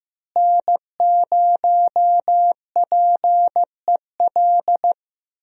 Morse code, heard as a keyed tone, about 15 wpm, 700 Hz. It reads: N0PEL
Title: USA Call Signs - 5 Characters - Repeated 3x - 15wpm